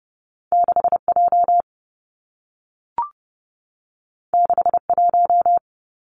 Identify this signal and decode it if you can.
Morse code, heard as a keyed tone, about 30 wpm, 700 Hz. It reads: 6 J 6 1
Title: Two Challenging Letter-Number Combinations 30wpm